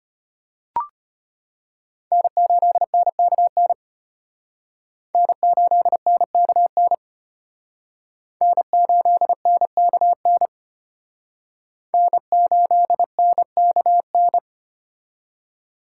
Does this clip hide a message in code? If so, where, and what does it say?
Yes: Morse code from 2.1 to 14.4 s, heard as a keyed tone, about 32 wpm, 700 Hz. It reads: N8NKN N8NKN N8NKN N8NKN